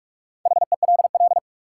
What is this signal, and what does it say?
Morse code, heard as a keyed tone, about 46 wpm, 700 Hz. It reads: HELL
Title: Sets of 5 Words Encore - New Word Drill - 45wpm